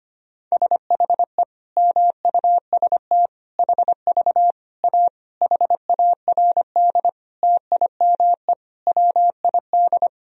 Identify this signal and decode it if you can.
Morse code, heard as a keyed tone, about 25 wpm, 700 Hz. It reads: SHE MUST HV A HARD TIME WID
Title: Sentences from Top 300 Words and Abbreviations 25wpm